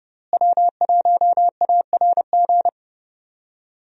Morse code, heard as a keyed tone, about 30 wpm, 700 Hz. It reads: W1ARG
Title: USA Call Signs - 5 Characters - Repeated 2x - 30wpm